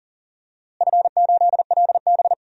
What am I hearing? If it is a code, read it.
Morse code, heard as a keyed tone, about 40 wpm, 700 Hz. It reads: F8LB
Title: DXCC-100 Call Signs - 4 Characters - Repeated 3x - 40wpm